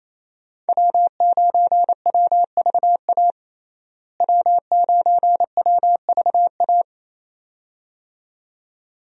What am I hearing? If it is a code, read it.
Morse code, heard as a keyed tone, about 28 wpm, 700 Hz. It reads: W9WVA W9WVA
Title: USA Call Signs - 5 Characters - Repeated 2x - 28wpm